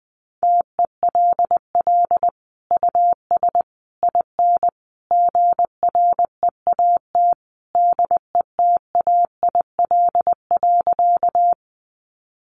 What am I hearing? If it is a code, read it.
Morse code, heard as a keyed tone, about 20 wpm, 700 Hz. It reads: TELL US IN GREAT DETAIL.
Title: Sentences from Top 1500 Words 20wpm